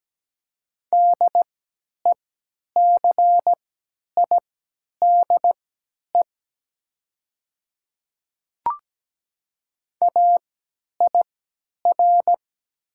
Morse code, heard as a keyed tone, about 17 wpm, spaced as for 10 wpm, 700 Hz. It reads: DECIDE AIR
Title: 3x Character Spacing - Top 500 Words - 17wpm